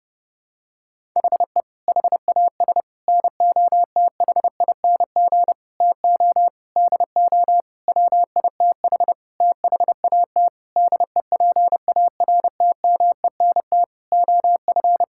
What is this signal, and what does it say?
Morse code, heard as a keyed tone, about 30 wpm, 700 Hz. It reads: HE HAS NOTHING TO DO WITH THAT DEPARTMENT OF
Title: Sentences from Top 1000 Words - Review Part 5 - 30wpm